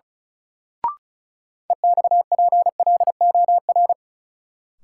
Morse code, heard as a keyed tone, about 36 wpm, 700 Hz. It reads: EXPLOR
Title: Sentences from Top 1100 Words - New Word Drill - 35wpm